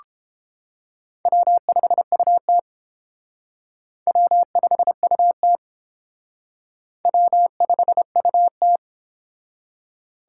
Morse code, heard as a keyed tone, about 29 wpm, 700 Hz. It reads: W5UT W5UT W5UT